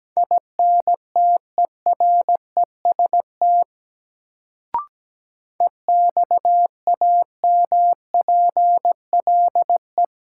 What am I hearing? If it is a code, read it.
Morse code, heard as a keyed tone, about 17 wpm, 700 Hz. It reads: INTEREST EXAMPLE
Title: Sentences from Top 400 Words - New Word Drill - 17wpm